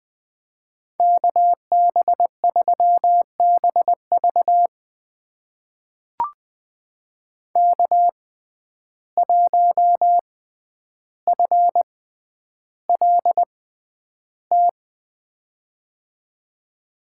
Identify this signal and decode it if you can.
Morse code, heard as a keyed tone, about 20 wpm, spaced as for 10 wpm, 700 Hz. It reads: KB3BV K1FLT
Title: USA Call Signs - 5 Characters - 6x Character Spacing - 20wpm